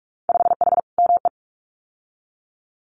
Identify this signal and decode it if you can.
Morse code, heard as a keyed tone, about 46 wpm, 700 Hz. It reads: 5H ME
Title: Winter Field Day Exchanges Encore 45wpm